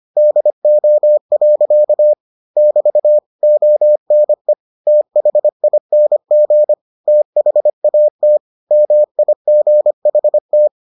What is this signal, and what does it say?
Morse code, heard as a keyed tone, about 25 wpm, 600 Hz. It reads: DO. <BT> ONE THING THAT MIGHT